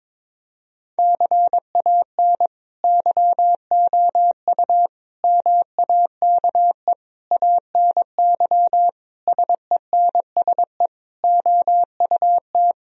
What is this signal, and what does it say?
Morse code, heard as a keyed tone, about 22 wpm, 700 Hz. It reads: CAN YOU MAKE ANY SENSE OUT
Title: Sentences from Top 1000 Words - Review Part 4 - 22wpm